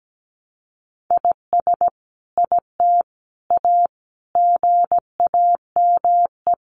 Morse code, heard as a keyed tone, about 17 wpm, 700 Hz. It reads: IS IT A GAME
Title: Sentences from Top 1200 Words 17wpm